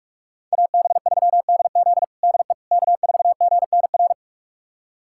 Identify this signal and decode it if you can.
Morse code, heard as a keyed tone, about 42 wpm, 700 Hz. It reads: AB3DZ DE K4GNR
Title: USA Callsign DE Callsign - 5 Characters - 45wpm